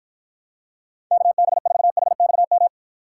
Morse code, heard as a keyed tone, about 53 wpm, 700 Hz. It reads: KB4HXM